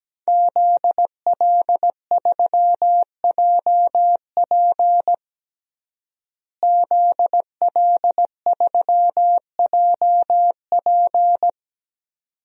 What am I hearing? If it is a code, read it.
Morse code, heard as a keyed tone, about 17 wpm, 700 Hz. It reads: ZL3JP ZL3JP
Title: DXCC-100 Call Signs - 5 Characters - Repeated 2x - 17wpm